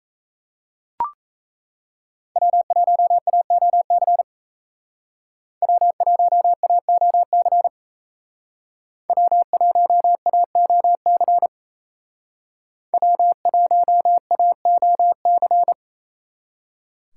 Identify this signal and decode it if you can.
Morse code, heard as a keyed tone, about 34 wpm, 700 Hz. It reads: W1AOC W1AOC W1AOC W1AOC
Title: Speed Racing - USA 1x3 Call Signs 28wpm